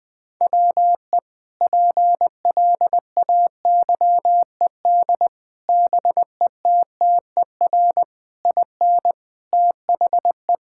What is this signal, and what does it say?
Morse code, heard as a keyed tone, about 20 wpm, 700 Hz. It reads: WE PLAYED BETTER IN THE
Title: Sentences with 5-Word Phrases - 20wpm